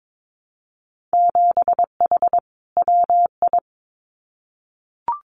Morse code, heard as a keyed tone, about 22 wpm, 700 Hz. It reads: 7H WI
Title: Winter Field Day Exchanges Encore 22wpm